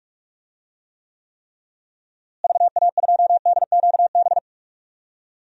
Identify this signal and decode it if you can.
Morse code, heard as a keyed tone, about 41 wpm, 700 Hz. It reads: VA2DQB
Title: DXCC-100 Call Signs - 6 Characters - Repeated 3x - 45wpm